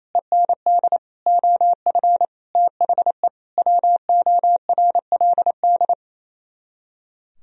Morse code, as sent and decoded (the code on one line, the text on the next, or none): . -. -.. / --- ..-. / - .... / .-- --- .-. .-.. -..
END OF THE WORLD